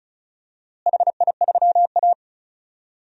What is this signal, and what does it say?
Morse code, heard as a keyed tone, about 34 wpm, 700 Hz. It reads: HI3A
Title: DXCC-100 Call Signs - 4 Characters - Repeated 2x - 35wpm